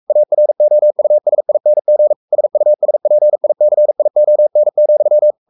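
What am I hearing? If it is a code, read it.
Morse code, heard as a keyed tone, about 44 wpm, 600 Hz. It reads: AROUSING SUSPICION,
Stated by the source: Morse code at 43 wpm sending Ken Follett's novel Never in English, from CW med Gnister.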